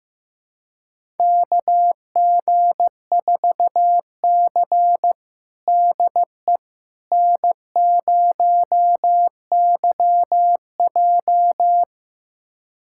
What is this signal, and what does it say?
Morse code, heard as a keyed tone, about 15 wpm, 700 Hz. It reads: KG4C DE N0YJ
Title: USA Callsign DE Callsign - 4 Characters - 15wpm